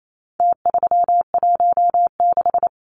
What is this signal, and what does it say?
Morse code, heard as a keyed tone, about 28 wpm, 700 Hz. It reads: T316
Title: Numbers - Contest Style - 4-Digit 28wpm